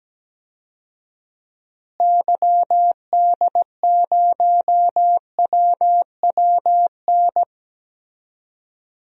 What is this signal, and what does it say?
Morse code, heard as a keyed tone, about 17 wpm, 700 Hz. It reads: YD0WWN
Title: DXCC-100 Call Signs - 6 Characters - Repeated 3x - 17wpm